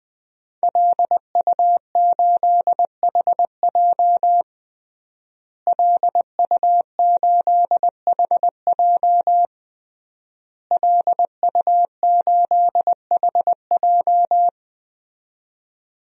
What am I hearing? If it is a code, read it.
Morse code, heard as a keyed tone, about 21 wpm, 700 Hz. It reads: LU8HJ LU8HJ LU8HJ
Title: DXCC-100 Call Signs - 5 Characters - Repeated 3x - 20wpm